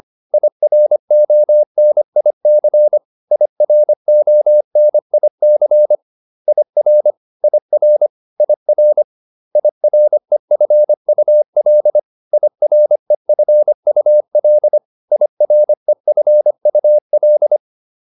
Morse code, heard as a keyed tone, about 25 wpm, 600 Hz. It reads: IRONIC IRONIC IR IR IR IREFUL IREFUL IREFUL
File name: IR6 25